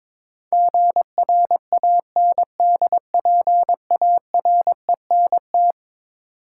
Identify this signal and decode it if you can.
Morse code, heard as a keyed tone, about 22 wpm, 700 Hz. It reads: GRANDPARENT
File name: practice / 22wpm / Eleven Letter Words - 22wpm